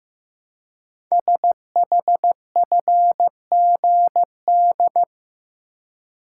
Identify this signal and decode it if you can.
Morse code, heard as a keyed tone, about 15 wpm, 700 Hz. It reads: SHFGD
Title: Code Groups - 5 Letters - 15wpm